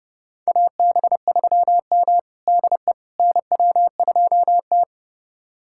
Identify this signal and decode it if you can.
Morse code, heard as a keyed tone, about 31 wpm, 700 Hz. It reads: AB3M DE NW2T
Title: USA Callsign DE Callsign - 4 Characters - 30wpm